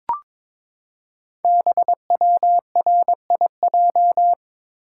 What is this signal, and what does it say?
Morse code, heard as a keyed tone, about 22 wpm, 700 Hz. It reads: BWRIJ
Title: Code Groups - 5 Letters - 22wpm